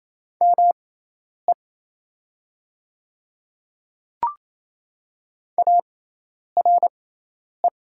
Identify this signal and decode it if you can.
Morse code, heard as a keyed tone, about 28 wpm, spaced as for 10 wpm, 700 Hz. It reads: ME ARE
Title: 6x Character Spacing - Top 500 Words - 28wpm